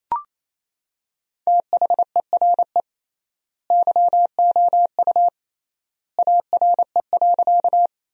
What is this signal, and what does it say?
Morse code, heard as a keyed tone, about 28 wpm, 700 Hz. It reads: THERE YOU ARE.
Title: 3x Word-Spacing - Sentences from Top 100 Words - 28wpm